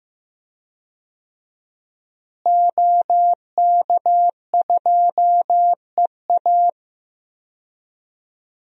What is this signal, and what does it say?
Morse code, heard as a keyed tone, about 15 wpm, 700 Hz. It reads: OK2EA
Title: DXCC-100 Call Signs - 5 Characters - Repeated 2x - 15wpm